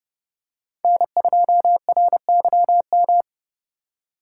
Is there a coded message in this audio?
Yes: Morse code from 0.8 to 3.2 s, heard as a keyed tone, about 30 wpm, 700 Hz. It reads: N2RYM